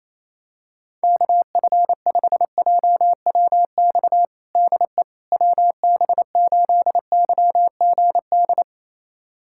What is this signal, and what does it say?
Morse code, heard as a keyed tone, about 28 wpm, 700 Hz. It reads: KF5JWX DE WB8YGD
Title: USA Callsign DE Callsign 28wpm